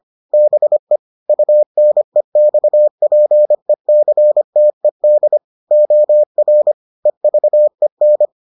Morse code, heard as a keyed tone, about 25 wpm, 600 Hz. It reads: BE UNEXPECTED OR EVEN